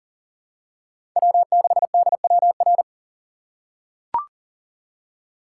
Morse code, heard as a keyed tone, about 40 wpm, 700 Hz. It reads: W6DWR
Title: USA Call Signs - 5 Characters - 2x Character Spacing - 40wpm